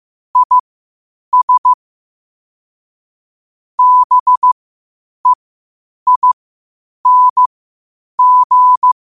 Morse code, heard as a keyed tone, about 15 wpm, spaced as for 8 wpm, 1000 Hz. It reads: IS BEING
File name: qotdfiles / 8wpm